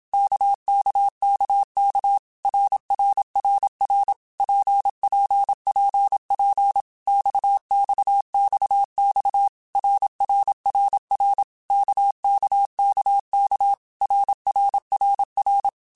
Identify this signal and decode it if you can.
Morse code, heard as a keyed tone, about 27 wpm, 800 Hz. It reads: KKKK RRRR PPPP XXXX RRRR KKKK RRRR
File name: Basic Letter Groups By7Groups / RKPX / RKPX@25